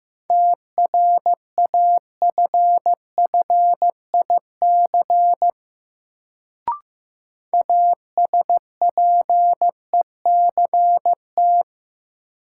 Morse code, heard as a keyed tone, about 15 wpm, 700 Hz. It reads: TRAFFIC ASPECT
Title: Sentences from Top 1300 Words - New Word Drill Alt - 15wpm